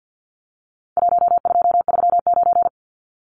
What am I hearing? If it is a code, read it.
Morse code, heard as a keyed tone, about 50 wpm, 700 Hz. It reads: 1239
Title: Numbers - 4-Digit 50wpm